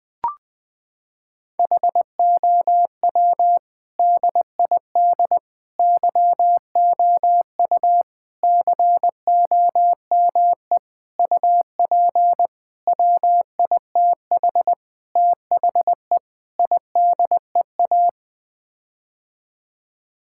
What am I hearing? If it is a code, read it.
Morse code, heard as a keyed tone, about 20 wpm, 700 Hz. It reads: HOW DID YOU COME UP WITH THE IDEA
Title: Sets of 8 Words - 20wpm